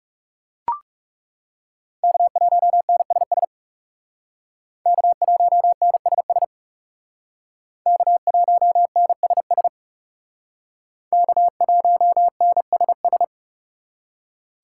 Morse code, heard as a keyed tone, about 36 wpm, 700 Hz. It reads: K1NSS K1NSS K1NSS K1NSS
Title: Speed Racing - USA 1x3 Call Signs 30wpm